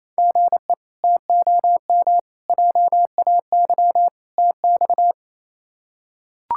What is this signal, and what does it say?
Morse code, heard as a keyed tone, about 28 wpm, 700 Hz. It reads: GE TOM JAY TX
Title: K1USN SST QSO Practice - 28wpm